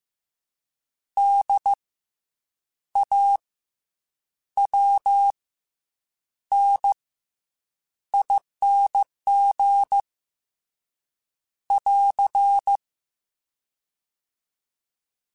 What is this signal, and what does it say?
Morse code, heard as a keyed tone, about 15 wpm, 800 Hz. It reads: D A W N ING <AR>
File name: Special INGs / Special INGs-mp3s / SlowToFaster ing @13